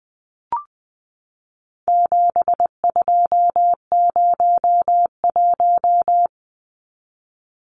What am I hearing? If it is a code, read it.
Morse code, heard as a keyed tone, about 20 wpm, 700 Hz. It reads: 7201
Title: Numbers - 4-Digit 20wpm